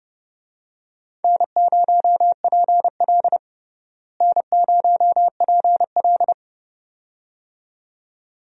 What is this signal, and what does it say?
Morse code, heard as a keyed tone, about 32 wpm, 700 Hz. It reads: N0PL N0PL
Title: USA Call Signs - 4 Characters - Repeated 2x - 30wpm